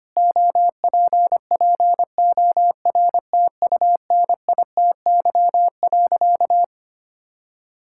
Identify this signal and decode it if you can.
Morse code, heard as a keyed tone, about 25 wpm, 700 Hz. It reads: OPPORTUNITY.
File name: practice / 25wpm / Sentences with 4-Word Phrases - 25wpm